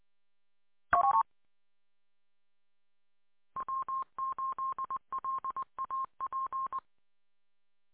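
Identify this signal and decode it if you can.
Morse code, heard as a keyed tone, about 20 wpm, 1060 Hz. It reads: N W8LAP
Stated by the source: Recorded on W8LAP 146.620 repeater audio archive.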